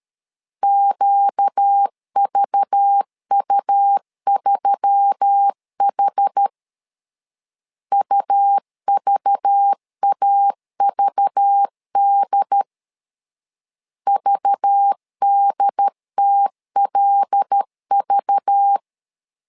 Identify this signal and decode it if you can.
Morse code, heard as a keyed tone, about 13 wpm, 800 Hz. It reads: QVU3H UVAVD VDTLV
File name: LEZIONI / 5 LBVC / 5 LBV 12wpm 1855